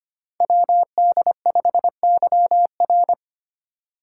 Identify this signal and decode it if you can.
Morse code, heard as a keyed tone, about 25 wpm, 700 Hz. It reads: WD5YR
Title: Code Groups - 5 Letter:Numbers - 25wpm